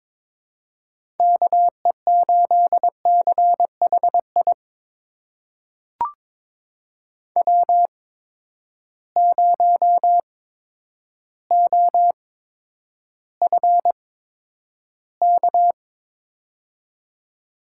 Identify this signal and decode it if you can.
Morse code, heard as a keyed tone, about 22 wpm, spaced as for 10 wpm, 700 Hz. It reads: KE8CHI W0OFK
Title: USA Call Signs - 8x Character Spacing - 22wpm